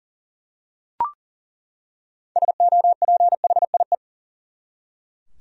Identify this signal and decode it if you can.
Morse code, heard as a keyed tone, about 40 wpm, 700 Hz. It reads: SOPHIE
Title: USA Common Names - 40wpm